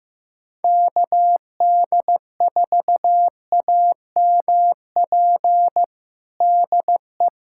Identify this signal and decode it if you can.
Morse code, heard as a keyed tone, about 15 wpm, 700 Hz. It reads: KD4AMP DE
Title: USA Callsign DE Callsign 15wpm